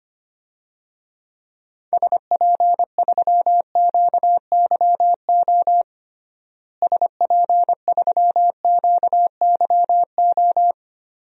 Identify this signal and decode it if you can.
Morse code, heard as a keyed tone, about 25 wpm, 700 Hz. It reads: SP3QYO SP3QYO
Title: DXCC-100 Call Signs - 6 Characters - Repeated 3x - 25wpm